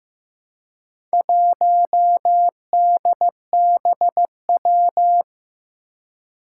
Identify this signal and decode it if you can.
Morse code, heard as a keyed tone, about 15 wpm, 700 Hz. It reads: 1DBW
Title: Four Challenging Letter-Number Combinations 15wpm